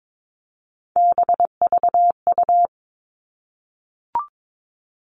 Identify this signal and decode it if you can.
Morse code, heard as a keyed tone, about 22 wpm, 700 Hz. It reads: BVU
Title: All Cut Numbers - 3-Digit 22wpm